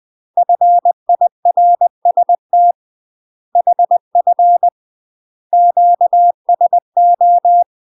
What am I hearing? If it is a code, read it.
Morse code, heard as a keyed tone, about 20 wpm, 700 Hz. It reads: FIRST HF QSO
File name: UK MASTER QSOs TRAINING / 20WPM / UK MASTER QSOs 0066